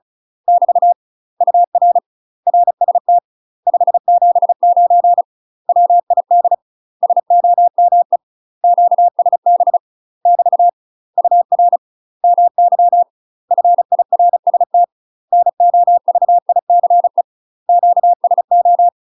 Morse code, heard as a keyed tone, about 34 wpm, 700 Hz. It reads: <BT> UR RST 579 WID SOME QSB <BT> UR MY FIRST NOVICE QSO
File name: UK MASTER QSOs TRAINING/35WPM